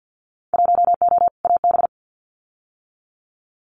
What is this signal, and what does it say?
Morse code, heard as a keyed tone, about 50 wpm, 700 Hz. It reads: 1O AB
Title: Winter Field Day Exchanges Encore 50wpm